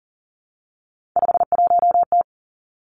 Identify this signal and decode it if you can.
Morse code, heard as a keyed tone, about 41 wpm, 700 Hz. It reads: 51T
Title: Three Challenging Letter-Number Combinations 40wpm